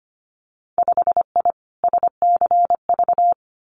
Morse code, heard as a keyed tone, about 25 wpm, 700 Hz. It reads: 5I SCV
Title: Winter Field Day Exchanges Encore 25wpm